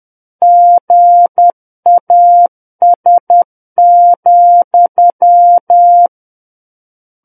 Morse code, heard as a keyed tone, about 10 wpm, 700 Hz. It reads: GAS,
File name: Book One / 01-Chapter 1 - 10 WPM